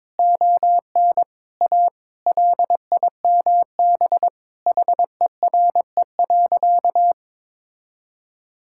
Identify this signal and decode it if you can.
Morse code, heard as a keyed tone, about 22 wpm, 700 Hz. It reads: ON A LIMB HERE.
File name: practice / 22wpm / Sentences with 8-Word Phrases - 22wpm